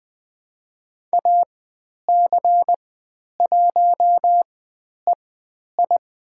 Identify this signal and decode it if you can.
Morse code, heard as a keyed tone, about 20 wpm, spaced as for 10 wpm, 700 Hz. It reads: AC1EI
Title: 109 Lesson 18g - Letter m - Call Signs 20wpm